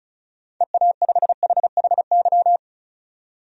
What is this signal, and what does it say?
Morse code, heard as a keyed tone, about 36 wpm, 700 Hz. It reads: EA5HHY
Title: DXCC-100 Call Signs - 6 Characters - 35wpm